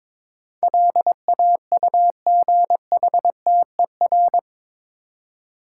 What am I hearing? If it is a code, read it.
Morse code, heard as a keyed tone, about 22 wpm, 700 Hz. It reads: LAUGHTER
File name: practice / 22wpm / Eight Letter Words - 22wpm